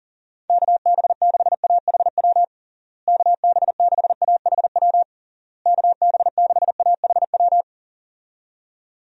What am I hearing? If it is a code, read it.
Morse code, heard as a keyed tone, about 40 wpm, 700 Hz. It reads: KB6AHW KB6AHW KB6AHW
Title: USA Call Signs - 6 Characters - Repeated 3x - 40wpm